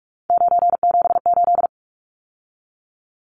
Morse code, heard as a keyed tone, about 44 wpm, 700 Hz. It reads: 978
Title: Numbers - 3-Digit 45wpm